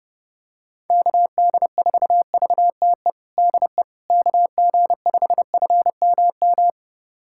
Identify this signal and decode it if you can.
Morse code, heard as a keyed tone, about 30 wpm, 700 Hz. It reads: KD4VTE DE KG5FMM
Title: USA Callsign DE Callsign - 6 Characters - 30wpm